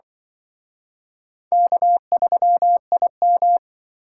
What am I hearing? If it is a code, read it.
Morse code, heard as a keyed tone, about 24 wpm, 700 Hz. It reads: K3IM
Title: Speed Racing - USA 1x2 Call Signs 20wpm